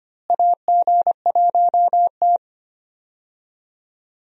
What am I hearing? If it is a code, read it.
Morse code, heard as a keyed tone, about 25 wpm, 700 Hz. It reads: AG1T
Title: USA Call Signs - 4 Characters - Repeated 3x - 25wpm